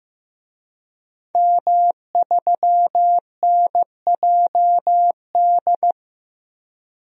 Morse code, heard as a keyed tone, about 15 wpm, 700 Hz. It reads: M3NJD
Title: DXCC-100 Call Signs - 5 Characters - Repeated 2x - 15wpm